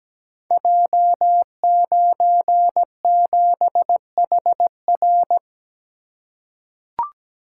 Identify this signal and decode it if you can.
Morse code, heard as a keyed tone, about 17 wpm, 700 Hz. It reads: J97HR
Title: Code Groups - 5 Letter:Numbers - 17wpm